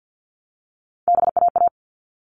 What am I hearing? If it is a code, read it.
Morse code, heard as a keyed tone, about 50 wpm, 700 Hz. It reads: BAA
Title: All Cut Numbers - 3-Digit 50wpm